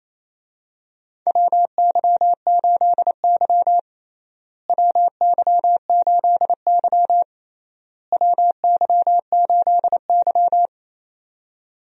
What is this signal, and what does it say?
Morse code, heard as a keyed tone, about 28 wpm, 700 Hz. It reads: WY8Y WY8Y WY8Y
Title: USA Call Signs - 4 Characters - Repeated 3x - 28wpm